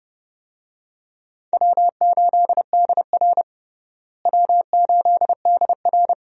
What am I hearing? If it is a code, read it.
Morse code, heard as a keyed tone, about 30 wpm, 700 Hz. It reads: W8DR W8DR
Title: USA Call Signs - 4 Characters - Repeated 2x - 30wpm